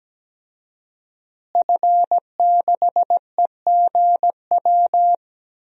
Morse code, heard as a keyed tone, about 17 wpm, 700 Hz. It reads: F6EGW
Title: DXCC-100 Call Signs - 5 Characters - Repeated 3x - 17wpm